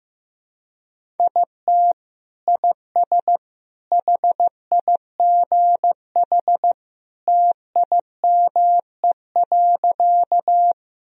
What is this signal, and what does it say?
Morse code, heard as a keyed tone, about 15 wpm, 700 Hz. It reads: IT IS HIGH TIME.
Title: Sentences from Top 1000 Words - Review Part 1 - 15wpm